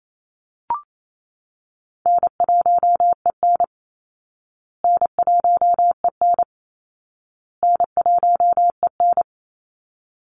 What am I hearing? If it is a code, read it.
Morse code, heard as a keyed tone, about 28 wpm, 700 Hz. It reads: N1EN N1EN N1EN
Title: Easy Call Signs Repeated 3x - 28wpm